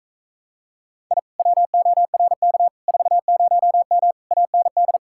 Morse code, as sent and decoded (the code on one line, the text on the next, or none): .. / .-- --- .-. -.- / ....- ----- -- / .- -. -..
I WORK 40M AND